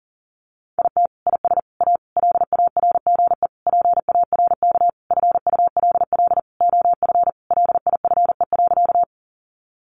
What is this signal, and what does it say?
Morse code, heard as a keyed tone, about 40 wpm, 700 Hz. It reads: IT IS A LARGE PARK FULL OF LIFE.
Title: Sentences from Top 1300 Words 40wpm